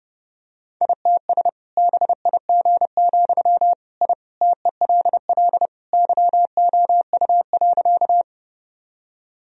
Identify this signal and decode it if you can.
Morse code, heard as a keyed tone, about 29 wpm, 700 Hz. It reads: ITS BIG, I TELL YOU.
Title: Sentences from Top 1000 Words - Review Part 1 - 30wpm